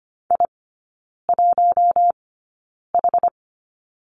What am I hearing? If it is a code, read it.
Morse code, heard as a keyed tone, about 25 wpm, spaced as for 9 wpm, 700 Hz. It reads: I1H
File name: Lesson 17e - Letter u - Three Characters 25wpm